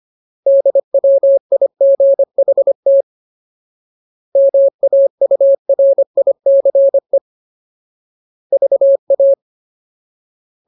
Morse code, heard as a keyed tone, about 25 wpm, 550 Hz. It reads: DWIGHT MAURICE VA
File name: Ultimate Word List / 25WPM Ultimate